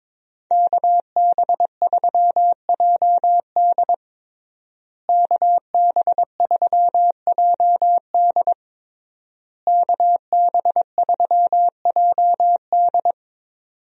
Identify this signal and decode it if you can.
Morse code, heard as a keyed tone, about 22 wpm, 700 Hz. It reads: KB3JD KB3JD KB3JD
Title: USA Call Signs - 5 Characters - Repeated 3x - 22wpm